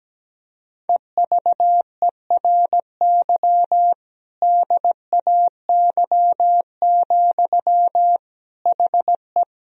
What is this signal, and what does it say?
Morse code, heard as a keyed tone, about 17 wpm, 700 Hz. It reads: EVERY DAY, HE
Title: Sentences from Top 1000 Words - Review Part 1 - 17wpm